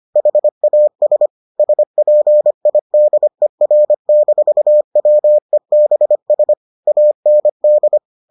Morse code, heard as a keyed tone, about 25 wpm, 600 Hz. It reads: HAS SPIDER-WEBS AND